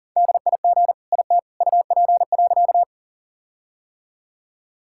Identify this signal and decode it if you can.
Morse code, heard as a keyed tone, about 40 wpm, 700 Hz. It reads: DIG IT UP.
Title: Sentences from Top 1600 Words 40wpm